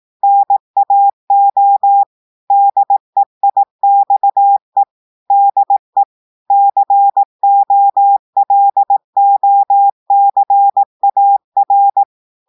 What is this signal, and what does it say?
Morse code, heard as a keyed tone, about 18 wpm, 800 Hz. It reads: NAO DEIXE DE COLOCAR
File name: audios / portugais / portugais texte en clair 2 vitesse 18 mots minutes